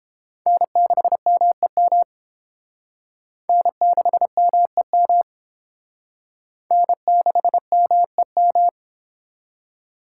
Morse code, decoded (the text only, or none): N6MEM N6MEM N6MEM